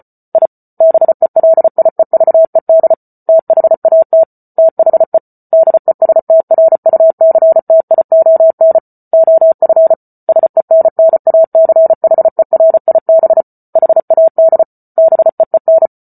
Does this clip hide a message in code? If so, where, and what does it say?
Yes: Morse code from 0.3 to 15.9 s, heard as a keyed tone, about 34 wpm, 650 Hz. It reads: I BELIEVED THAT THE DESTRUCTION OF SENNACHERIB HAD BEEN